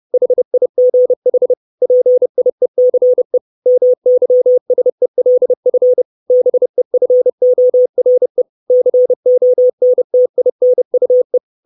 Morse code, heard as a keyed tone, about 30 wpm, 500 Hz. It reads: HIGH PIECE MYSELF BEFORE CONTINUE
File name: 632 Words 30WPM